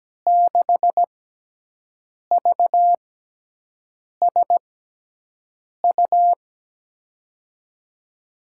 Morse code, heard as a keyed tone, about 17 wpm, spaced as for 6 wpm, 700 Hz. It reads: 6VSU
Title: USA Call Signs - 6x Character Spacing - 17wpm